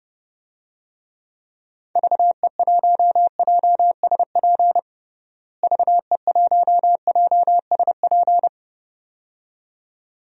Morse code, heard as a keyed tone, about 30 wpm, 700 Hz. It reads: VE1JSP VE1JSP
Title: DXCC-100 Call Signs - 6 Characters - Repeated 2x - 30wpm